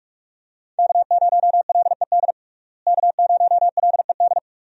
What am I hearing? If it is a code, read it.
Morse code, heard as a keyed tone, about 48 wpm, 700 Hz. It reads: K0LED K0LED
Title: USA Call Signs - 5 Characters - Repeated 2x - 45wpm